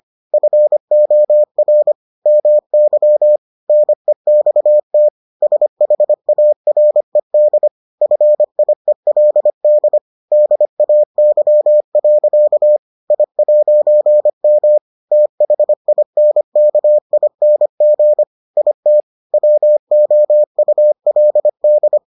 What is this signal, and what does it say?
Morse code, heard as a keyed tone, about 25 wpm, 600 Hz. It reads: FOR MY NEXT SHARED FIELD DAY. I'M THINKING IT WOULD